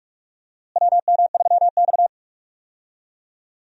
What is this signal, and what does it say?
Morse code, heard as a keyed tone, about 42 wpm, 700 Hz. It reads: WM3X
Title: USA Call Signs - 4 Characters - Repeated 3x - 45wpm